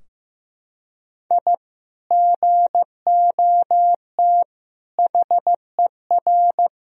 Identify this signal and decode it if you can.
Morse code, heard as a keyed tone, about 15 wpm, 700 Hz. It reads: I GOT HER
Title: Sentences from Top 1000 Words - Review Part 4 - 15wpm